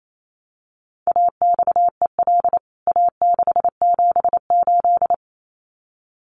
Morse code, heard as a keyed tone, about 28 wpm, 700 Hz. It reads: AXEL A678
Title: CWT Exchange 28wpm